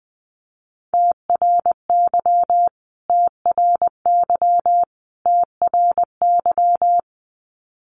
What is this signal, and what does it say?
Morse code, heard as a keyed tone, about 20 wpm, 700 Hz. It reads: TRY TRY TRY
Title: Top 500 Words Repeated 20wpm